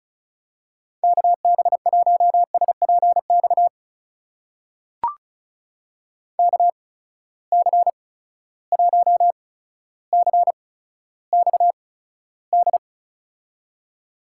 Morse code, heard as a keyed tone, about 36 wpm, spaced as for 15 wpm, 700 Hz. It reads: KB1SPX KC1CXD
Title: USA Call Signs - 6 Characters - 8x Character Spacing - 35wpm